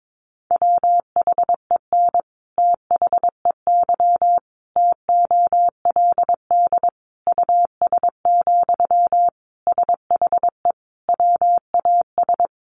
Morse code, heard as a keyed tone, about 22 wpm, 700 Hz. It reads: WHEN THEY TOLD US, SHE WAS
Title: Sentences from Top 400 Words 22wpm